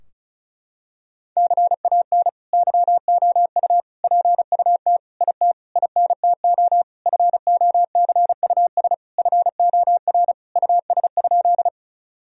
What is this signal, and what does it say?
Morse code, heard as a keyed tone, about 36 wpm, 700 Hz. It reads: CAN YOU PUT IT INTO FOCUS FOR US?
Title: Sentences from Top 1000 Words - Review Part 4 - 35wpm